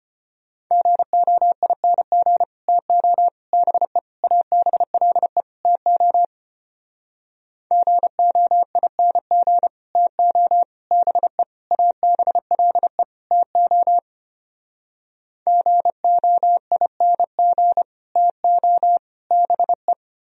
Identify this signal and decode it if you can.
Morse code, heard as a keyed tone, about 33 wpm, 700 Hz. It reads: GOING TO BE ABLE TO GOING TO BE ABLE TO GOING TO BE